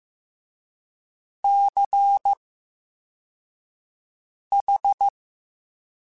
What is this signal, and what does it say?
Morse code, heard as a keyed tone, about 15 wpm, spaced as for 4 wpm, 800 Hz. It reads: CH
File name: Special INGs-mp3s / Special INGs-mp3s / VerySlowToFaster Ing @13